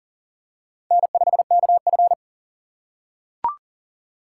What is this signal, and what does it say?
Morse code, heard as a keyed tone, about 40 wpm, 700 Hz. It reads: N5KF
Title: USA Call Signs - 4 Characters - 2x Character Spacing - 40wpm